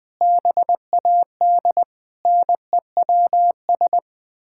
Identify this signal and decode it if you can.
Morse code, heard as a keyed tone, about 20 wpm, 700 Hz. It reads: BAD NEWS
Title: Sets of 9 Words - 20wpm